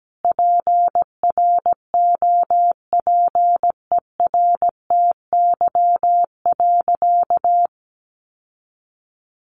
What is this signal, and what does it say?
Morse code, heard as a keyed tone, about 17 wpm, 700 Hz. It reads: PROPERTY.
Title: Sentences from Top 1000 Words 17wpm